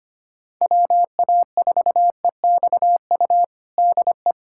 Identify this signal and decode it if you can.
Morse code, heard as a keyed tone, about 25 wpm, 700 Hz. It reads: WA4EXU DE
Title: USA Callsign DE Callsign - 6 Characters - 25wpm